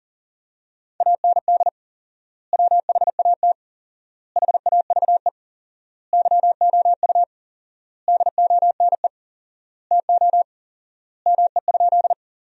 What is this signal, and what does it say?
Morse code, heard as a keyed tone, about 40 wpm, 700 Hz. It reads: AND WHAT HAVE YOU DONE TO ME?
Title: 4x Word-Spacing - Sentences from Top 500 Words - 40wpm